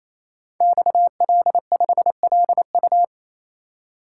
Morse code, heard as a keyed tone, about 27 wpm, 700 Hz. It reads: XL5LU
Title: Code Groups - 5 Letter:Numbers - 28wpm